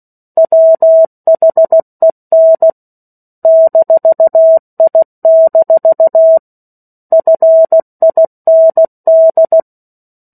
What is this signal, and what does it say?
Morse code, heard as a keyed tone, about 16 wpm, 650 Hz. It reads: WHEN -I- FIND
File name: 03-Chapter-03 16 WPM